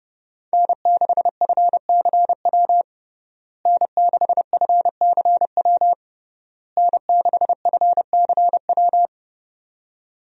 Morse code, heard as a keyed tone, about 32 wpm, 700 Hz. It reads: N6FCW N6FCW N6FCW
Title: USA Call Signs - 5 Characters - Repeated 3x - 30wpm